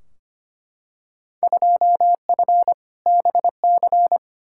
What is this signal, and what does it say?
Morse code, heard as a keyed tone, about 25 wpm, 700 Hz. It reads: 2F BC
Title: Field Day Exchanges Encore - 25wpm